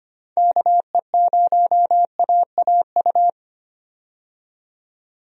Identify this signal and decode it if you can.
Morse code, heard as a keyed tone, about 25 wpm, 700 Hz. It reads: KE0AAU
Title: USA Call Signs - 6 Characters - 25wpm